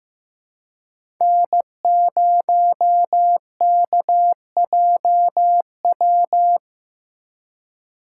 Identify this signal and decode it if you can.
Morse code, heard as a keyed tone, about 15 wpm, 700 Hz. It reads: N0KJW
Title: USA Call Signs - 4x Character Spacing - 15wpm